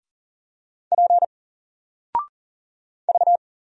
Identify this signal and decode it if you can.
Morse code, heard as a keyed tone, about 40 wpm, 700 Hz. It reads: P V
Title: Single Letter-Number 40wpm